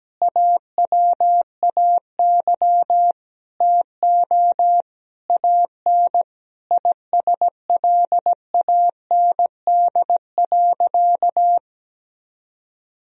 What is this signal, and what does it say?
Morse code, heard as a keyed tone, about 17 wpm, 700 Hz. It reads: AWAY TO AN ISLAND.
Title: Sentences from Top 1000 Words - Review Part 2 - 17wpm